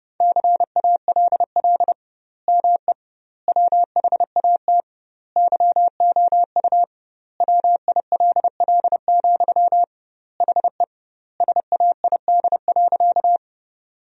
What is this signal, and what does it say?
Morse code, heard as a keyed tone, about 30 wpm, 700 Hz. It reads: CALL ME WHAT YOU WILL, HE SAID.
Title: 2x Word-Spacing - Sentences from Top 200 Words - 30wpm